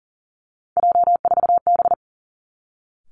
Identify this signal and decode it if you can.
Morse code, heard as a keyed tone, about 38 wpm, 700 Hz. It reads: J4B
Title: Three Challenging Letter-Number Combinations 40wpm